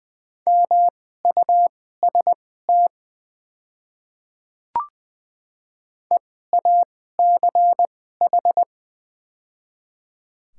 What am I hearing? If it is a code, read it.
Morse code, heard as a keyed tone, about 20 wpm, spaced as for 14 wpm, 700 Hz. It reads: MUST EACH
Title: 2x Character Spacing - Top 200 Words - 20wpm